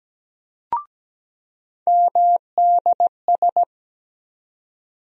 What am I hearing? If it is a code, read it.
Morse code, heard as a keyed tone, about 17 wpm, 700 Hz. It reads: MDS
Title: USA Call Sign Postfixes - 3 Letters 17wpm